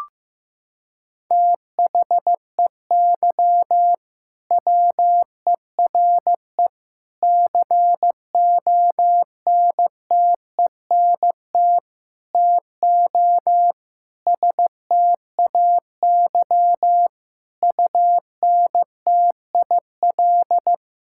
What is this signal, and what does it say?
Morse code, heard as a keyed tone, about 15 wpm, 700 Hz. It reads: THEY WERE CONTENT TO STAY UNTIL
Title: Sentences from Top 1000 Words - Review Part 6 - 15wpm